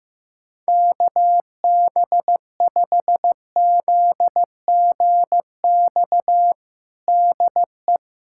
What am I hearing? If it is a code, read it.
Morse code, heard as a keyed tone, about 15 wpm, 700 Hz. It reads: KB5ZGX DE
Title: USA Callsign DE Callsign - 6 Characters - 15wpm